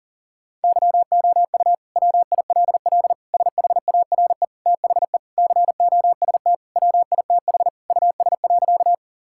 Morse code, heard as a keyed tone, about 40 wpm, 700 Hz. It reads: YOU WILL SHARE THE COST WITH US.
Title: Sentences from Top 1000 Words - Review Part 6 - 40wpm